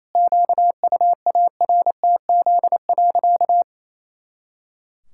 Morse code, heard as a keyed tone, about 28 wpm, 700 Hz. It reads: QUARTZ.
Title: Pangram Challenge v2 - 28wpm